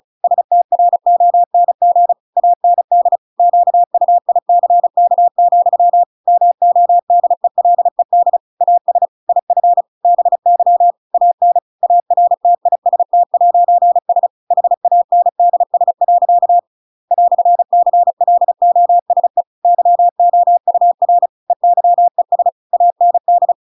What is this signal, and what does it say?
Morse code, heard as a keyed tone, about 34 wpm, 700 Hz. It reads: STRONG AND QUICK, MODELED AS IF BY AN ARTIST'S HANDS. "CLOSE YOUR EYES AND
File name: Pinocchio / Pinocchio ch8 35 35 ik1hge